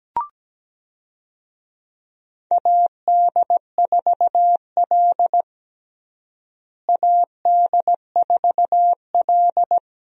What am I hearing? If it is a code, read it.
Morse code, heard as a keyed tone, about 17 wpm, 700 Hz. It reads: AD4L AD4L
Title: USA Call Signs - 4 Characters - Repeated 3x - 17wpm